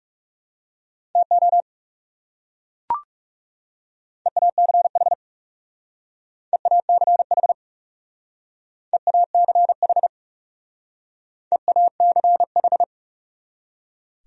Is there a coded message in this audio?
Yes: Morse code from 1.2 to 12.8 s, heard as a keyed tone, about 38 wpm, 700 Hz. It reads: TO EACH EACH EACH EACH